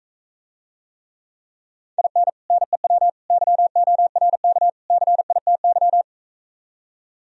Morse code, heard as a keyed tone, about 45 wpm, 700 Hz. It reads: IN NEW YORK CITY